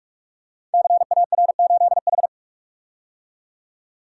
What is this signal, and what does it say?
Morse code, heard as a keyed tone, about 44 wpm, 700 Hz. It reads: CAR8H
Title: Code Groups - 5 Letter:Numbers - 45wpm